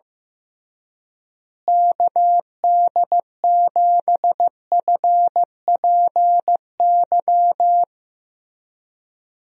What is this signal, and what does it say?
Morse code, heard as a keyed tone, about 15 wpm, 700 Hz. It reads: KD7FPY
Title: USA Call Signs - 6 Characters - Repeated 3x - 15wpm